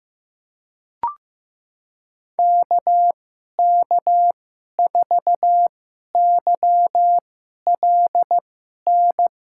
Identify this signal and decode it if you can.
Morse code, heard as a keyed tone, about 16 wpm, spaced as for 11 wpm, 700 Hz. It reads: KK4YLN
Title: USA Call Signs - 6 Characters - 2x Character Spacing - 15wpm